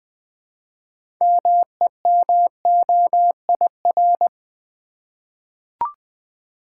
Morse code, heard as a keyed tone, about 20 wpm, 700 Hz. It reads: MEMOIR
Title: 6 Letter Words - First 3 Letters Given - 20wpm